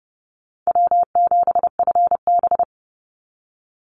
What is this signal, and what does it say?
Morse code, heard as a keyed tone, about 30 wpm, 700 Hz. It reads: W7FB
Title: Call Signs - 30wpm